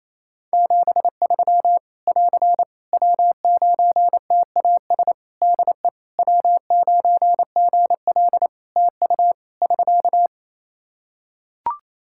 Morse code, heard as a keyed tone, about 27 wpm, 700 Hz. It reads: 73 <AR> W9TAS DE W9GL TU <SK>
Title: Practice QSO - Exchange 4 - Ending - 28wpm